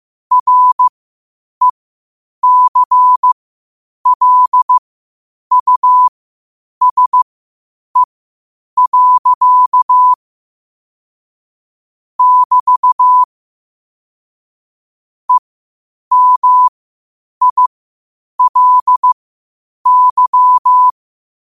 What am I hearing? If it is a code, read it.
Morse code, heard as a keyed tone, about 15 wpm, spaced as for 8 wpm, 1000 Hz. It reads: RECLUSE. <BT> EMILY
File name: qotdfiles / 8wpm